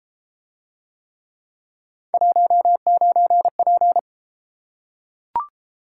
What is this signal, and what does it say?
Morse code, heard as a keyed tone, about 30 wpm, 700 Hz. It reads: 19P